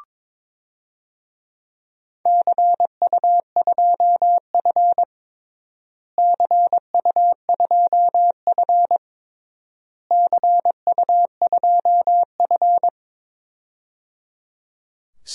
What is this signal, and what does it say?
Morse code, heard as a keyed tone, about 22 wpm, 700 Hz. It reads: CU2F CU2F CU2F
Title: DXCC-100 Call Signs - 4 Characters - Repeated 3x - 22wpm